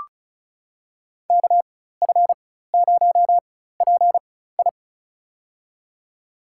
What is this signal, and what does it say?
Morse code, heard as a keyed tone, about 34 wpm, spaced as for 16 wpm, 700 Hz. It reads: KF0PI
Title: USA Call Signs - 5 Characters - 4x Character Spacing - 35wpm